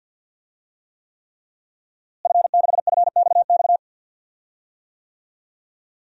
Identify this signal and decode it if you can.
Morse code, heard as a keyed tone, about 53 wpm, 700 Hz. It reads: F6FXX
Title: DXCC-100 Call Signs - 5 Characters - Repeated 3x - 50wpm